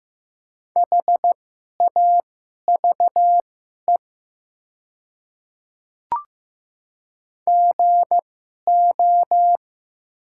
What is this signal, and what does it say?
Morse code, heard as a keyed tone, about 15 wpm, spaced as for 11 wpm, 700 Hz. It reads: HAVE GO
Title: 2x Character Spacing - Top 100 Words - 15wpm